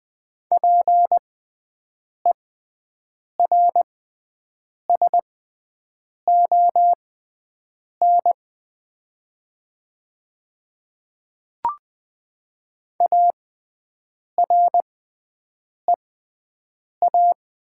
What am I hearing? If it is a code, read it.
Morse code, heard as a keyed tone, about 20 wpm, spaced as for 7 wpm, 700 Hz. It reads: PERSON AREA
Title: 6x Character Spacing - Top 500 Words - 20wpm